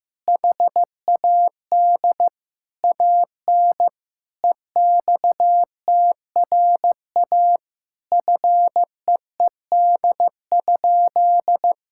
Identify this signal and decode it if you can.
Morse code, heard as a keyed tone, about 15 wpm, 700 Hz. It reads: HAD AN EXTRA FEED?
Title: Sentences from Top 1600 Words 15wpm